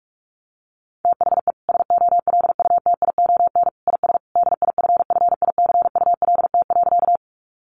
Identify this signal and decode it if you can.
Morse code, heard as a keyed tone, about 44 wpm, 700 Hz. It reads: THE SOLUTION IS DIFFICULT.
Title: Sentences from Top 1300 Words 45wpm